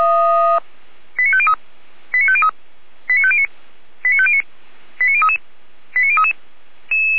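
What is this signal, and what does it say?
Five-tone selective call (ZVEI), heard as quick runs of five tones, tones of about 70 ms. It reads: 89503 89503 89509 89509 89033 89033
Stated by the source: Five-tone ZVEI selective call 89010, received on an SDR by fire brigade Feuerwehr Orlamünde.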